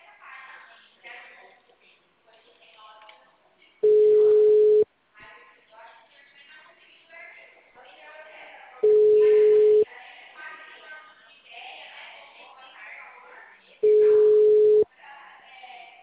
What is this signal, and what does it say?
Ringback tone (Europe, 425 Hz), with speech.